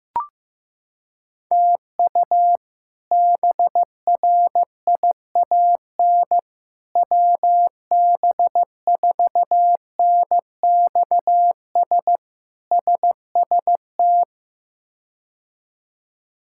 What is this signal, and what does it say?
Morse code, heard as a keyed tone, about 15 wpm, 700 Hz. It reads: TU BRIAN WB4NXS SST
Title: K1USN SST QSO Practice - 15wpm